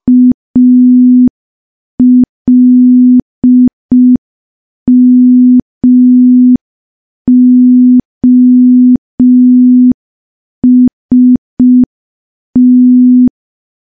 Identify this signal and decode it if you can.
Morse code, heard as a keyed tone, about 5 wpm, 250 Hz. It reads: ALMOST
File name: morse / morse-training-05WPM